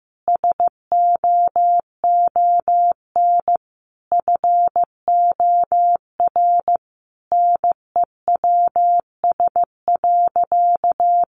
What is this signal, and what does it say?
Morse code, heard as a keyed tone, about 15 wpm, 700 Hz. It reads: SOON FOR NEWS.
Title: Sentences from Top 800 Words 15wpm